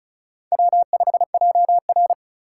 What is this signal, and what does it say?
Morse code, heard as a keyed tone, about 36 wpm, 700 Hz. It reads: W5JR
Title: USA Call Signs - 4 Characters - 3x Character Spacing - 35wpm